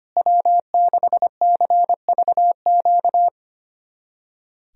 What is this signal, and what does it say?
Morse code, heard as a keyed tone, about 25 wpm, 700 Hz. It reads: W6CVQ
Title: USA Callsign DE Callsign 25wpm